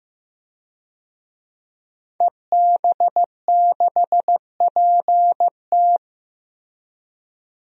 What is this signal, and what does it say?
Morse code, heard as a keyed tone, about 15 wpm, 700 Hz. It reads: EB6PT
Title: DXCC-100 Call Signs - 5 Characters - 15wpm